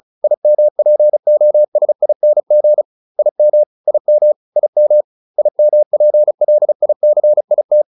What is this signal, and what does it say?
Morse code, heard as a keyed tone, about 34 wpm, 600 Hz. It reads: IMPOSING IM IM IM IMPLICIT